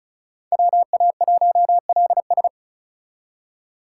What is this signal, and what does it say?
Morse code, heard as a keyed tone, about 46 wpm, 700 Hz. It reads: WA1LS